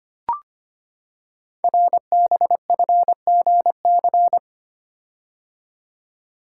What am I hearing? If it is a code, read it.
Morse code, heard as a keyed tone, about 25 wpm, 700 Hz. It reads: RBFGC
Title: Code Groups - 5 Letters - 25wpm